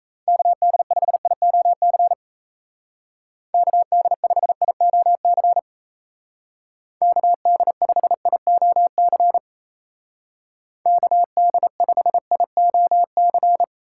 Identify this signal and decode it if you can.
Morse code, heard as a keyed tone, about 34 wpm, 700 Hz. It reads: KD5IOC KD5IOC KD5IOC KD5IOC
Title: Speed Racing - USA 2x3 Call Signs 28wpm